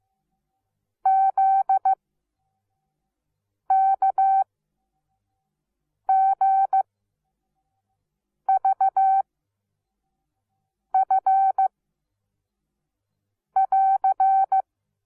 Morse code, heard as a keyed tone, about 15 wpm, spaced as for 5 wpm, 770 Hz. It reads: ZKGVF <AR>
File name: Onlinelehrgang / morsen / DJ4UF Morsen17